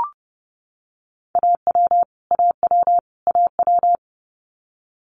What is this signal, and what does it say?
Morse code, heard as a keyed tone, about 30 wpm, 700 Hz. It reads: AW AW AW
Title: Multi-letter Phonemes Repeated 3x 30wpm